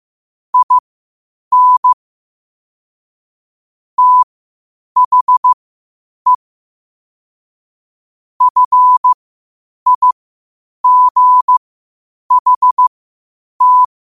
Morse code, heard as a keyed tone, about 15 wpm, spaced as for 8 wpm, 1000 Hz. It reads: IN THE FIGHT
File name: qotdfiles / 8wpm